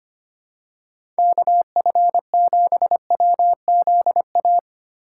Morse code, heard as a keyed tone, about 25 wpm, 700 Hz. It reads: KF7WZA
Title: USA Call Signs - 6 Characters - 8x Character Spacing - 25wpm